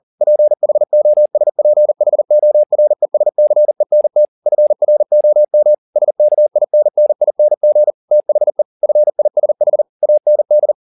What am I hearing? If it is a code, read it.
Morse code, heard as a keyed tone, about 40 wpm, 600 Hz. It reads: PHOSPHORESCENT FROM SKINNING THE FISH AND
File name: the old man and the sea 2 40wpm